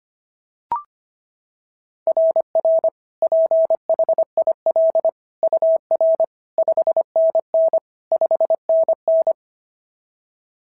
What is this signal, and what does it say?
Morse code, heard as a keyed tone, about 25 wpm, 660 Hz. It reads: RR PHIL UR 5NN 5NN
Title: SOTA QSO Practice 25wpm